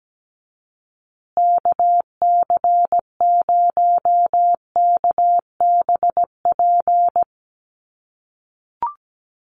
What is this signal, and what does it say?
Morse code, heard as a keyed tone, about 17 wpm, 700 Hz. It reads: KC0KBP
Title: USA Call Signs Repeated 3x 17wpm